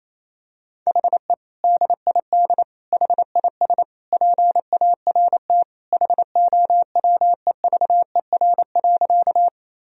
Morse code, heard as a keyed tone, about 28 wpm, 700 Hz. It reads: HE DID HIS PART HOWEVER.
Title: Sentences from Top 1000 Words - Review Part 1 - 28wpm